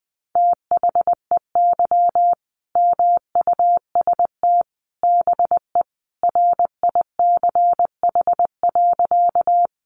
Morse code, heard as a keyed tone, about 20 wpm, 700 Hz. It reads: THEY MUST BE RICH.
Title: Sentences from Top 1100 Words 20wpm